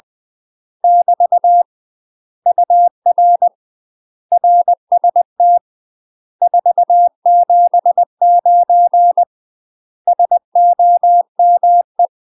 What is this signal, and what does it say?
Morse code, heard as a keyed tone, about 20 wpm, 700 Hz. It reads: <BT> UR RST 479 SOME
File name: UK MASTER QSOs TRAINING / 20WPM / UK MASTER QSOs 0002